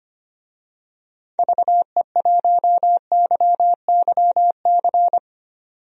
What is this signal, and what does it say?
Morse code, heard as a keyed tone, about 25 wpm, 690 Hz. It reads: VE1YYC
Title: DXCC-100 Call Signs - 6 Characters - 25wpm